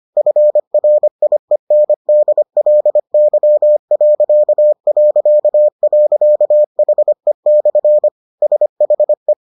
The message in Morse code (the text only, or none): FRIENDLY...HE/SHE